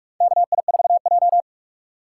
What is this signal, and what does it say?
Morse code, heard as a keyed tone, about 46 wpm, 700 Hz. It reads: KI4J
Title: USA Call Signs - 4 Characters - 2x Character Spacing - 45wpm